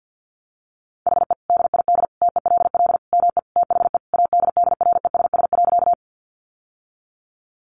Morse code, heard as a keyed tone, about 48 wpm, 700 Hz. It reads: HE DID TELL ME THE ADDRESS.
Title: Sentences from Top 1000 Words 50wpm